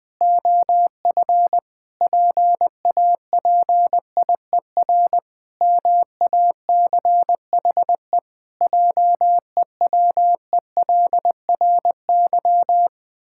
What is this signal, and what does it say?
Morse code, heard as a keyed tone, about 20 wpm, 700 Hz. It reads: OF PAPIER MACHE JEWELRY
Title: Pangram Challenge v2 - 20wpm